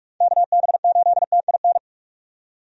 Morse code, heard as a keyed tone, about 46 wpm, 700 Hz. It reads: KB8TIN
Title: USA Call Signs - 6 Characters - 45wpm